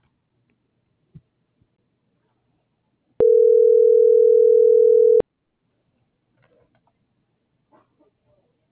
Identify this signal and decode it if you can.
Ringback tone (North America), 440 + 480 Hz.